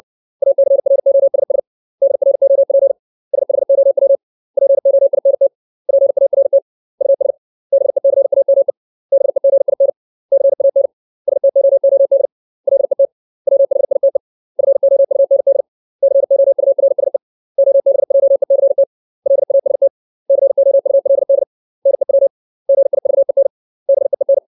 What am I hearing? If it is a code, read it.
Morse code, heard as a keyed tone, about 60 wpm, 550 Hz. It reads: ACROSS DROP SHOW POINT PART US BECAME BEGIN CAN STOOD LET WHITE FOUND COURSE OBJECT LAST COULD NEW GIVEN BEEN
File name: 632 words / 632 Words 60WPM